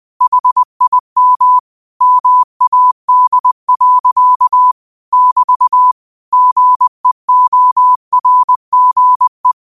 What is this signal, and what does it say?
Morse code, heard as a keyed tone, about 20 wpm, 1000 Hz. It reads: HIM MAD. <BT> GEORGE
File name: qotdfiles / 20wpm